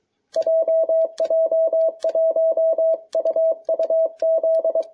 Morse code, heard as a keyed tone, about 22 wpm, 660 Hz. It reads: JJ1UUZ